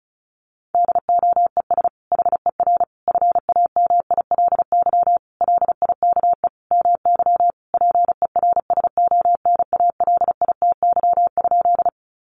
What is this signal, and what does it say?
Morse code, heard as a keyed tone, about 34 wpm, 700 Hz. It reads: DOES HER FAMILY LIKE MY PERSONALITY?
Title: Sentences from Top 1400 Words 35wpm